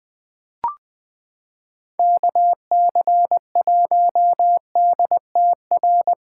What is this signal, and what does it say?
Morse code, heard as a keyed tone, about 20 wpm, 700 Hz. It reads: KC1DTR
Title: USA Callsign DE Callsign 20wpm